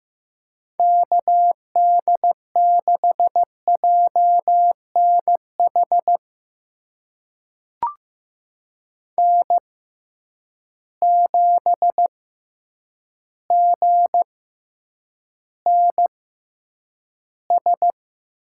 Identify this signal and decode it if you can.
Morse code, heard as a keyed tone, about 15 wpm, spaced as for 8 wpm, 700 Hz. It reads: KD6JNH N7GNS
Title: USA Call Signs - 6x Character Spacing - 15wpm